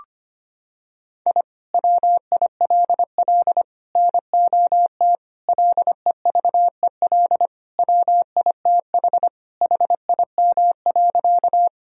Morse code, heard as a keyed tone, about 25 wpm, 700 Hz. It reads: I WILL NOT LEVEL WITH HIM.
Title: Sentences from Top 1000 Words - Review Part 4 - 25wpm